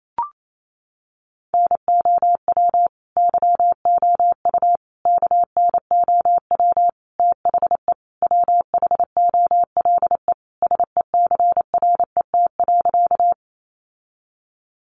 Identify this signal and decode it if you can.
Morse code, heard as a keyed tone, about 28 wpm, 700 Hz. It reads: NOW YOU KNOW THE WHOLE SECRET.
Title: Sentences from Top 1200 Words 28wpm